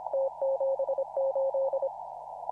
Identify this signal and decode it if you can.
Morse code, heard as a keyed tone, about 26 wpm, 540 Hz. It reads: T78
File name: CW / RHOB CW 2017 HA6P 0001